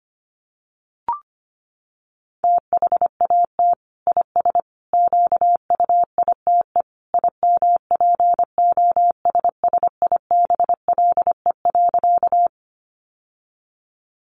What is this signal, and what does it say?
Morse code, heard as a keyed tone, about 25 wpm, 700 Hz. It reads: THAT IS QUITE IMPOSSIBLE.
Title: Sentences from Top 1100 Words 25wpm